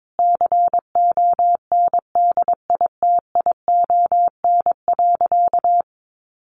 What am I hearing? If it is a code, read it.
Morse code, heard as a keyed tone, about 22 wpm, 700 Hz. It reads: CONDITION.
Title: Sentences from Top 1000 Words 22wpm